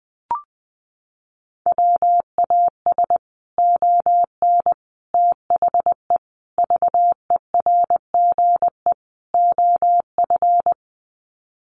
Morse code, heard as a keyed tone, about 19 wpm, 700 Hz. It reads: WAS ON THE VERGE OF
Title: Sets of 5 Words Encore 20wpm